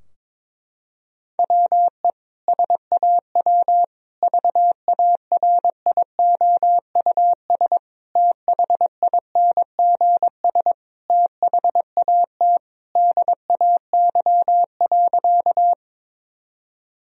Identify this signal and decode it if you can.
Morse code, heard as a keyed tone, about 22 wpm, 700 Hz. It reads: WE SAW VARIOUS THINGS THAT DAY.
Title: Sentences from Top 1000 Words - Review Part 4 - 22wpm